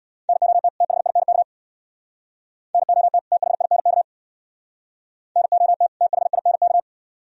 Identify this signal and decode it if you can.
Morse code, heard as a keyed tone, about 64 wpm, 700 Hz. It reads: NOT THINK NOT THINK NOT THINK